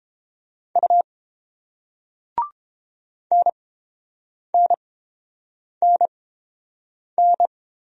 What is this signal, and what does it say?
Morse code, heard as a keyed tone, about 31 wpm, spaced as for 7 wpm, 700 Hz. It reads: U NNNN